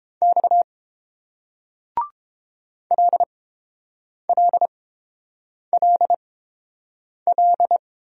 Morse code, heard as a keyed tone, about 29 wpm, spaced as for 7 wpm, 700 Hz. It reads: X LLLL